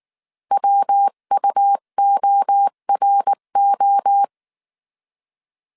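Morse code, heard as a keyed tone, about 19 wpm, 800 Hz. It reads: WUORO
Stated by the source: Morse code dictation at 18 wpm from the IW3SGT QRS Morse code training course, five-character groups of 300 characters.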